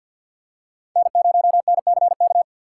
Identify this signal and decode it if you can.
Morse code, heard as a keyed tone, about 49 wpm, 700 Hz. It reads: N0NCK
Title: USA Call Signs - 5 Characters - 6x Character Spacing - 50wpm